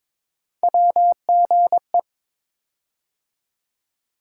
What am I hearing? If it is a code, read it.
Morse code, heard as a keyed tone, about 22 wpm, 700 Hz. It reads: WGE
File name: practice / 22wpm / Three Letter-Number Combinations - 22wpm